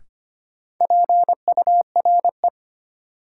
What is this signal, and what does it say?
Morse code, heard as a keyed tone, about 25 wpm, 700 Hz. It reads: PURE
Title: Sentences from Top 1500 Words - New Word Drill - 25wpm